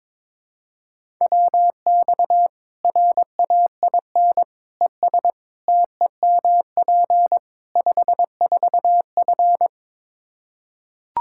Morse code, heard as a keyed tone, about 21 wpm, 700 Hz. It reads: WX RAIN ES TEMP 54F
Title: Practice QSO - Exchange 2 - Info - 22wpm